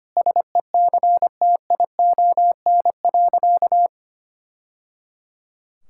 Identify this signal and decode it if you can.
Morse code, heard as a keyed tone, about 25 wpm, 700 Hz. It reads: SECTION.
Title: Sentences from Top 1000 Words - Review Part 5 - 25wpm